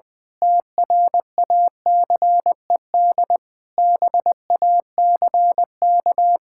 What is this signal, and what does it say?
Morse code, heard as a keyed tone, about 21 wpm, 700 Hz. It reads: TRACED BACK
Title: Sentences with 6-Word Phrases - Spoken Beginning - 20wpm